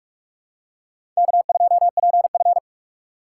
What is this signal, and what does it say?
Morse code, heard as a keyed tone, about 46 wpm, 700 Hz. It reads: K2PF
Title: USA Call Signs - 4 Characters - 4x Character Spacing - 45wpm